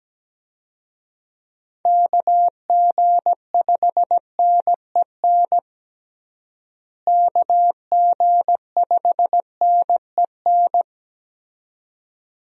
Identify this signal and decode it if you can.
Morse code, heard as a keyed tone, about 17 wpm, 700 Hz. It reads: KG5NEN KG5NEN
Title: USA Call Signs - 6 Characters - Repeated 2x - 17wpm